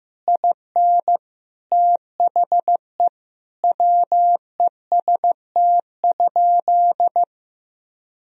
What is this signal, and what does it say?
Morse code, heard as a keyed tone, about 15 wpm, 700 Hz. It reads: IN THE WEST?
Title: Sentences from Top 1000 Words - Review Part 2 - 15wpm